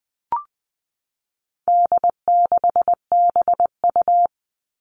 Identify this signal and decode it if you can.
Morse code, heard as a keyed tone, about 20 wpm, 700 Hz. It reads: D6BU
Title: All Cut Numbers - 4-Digit 20wpm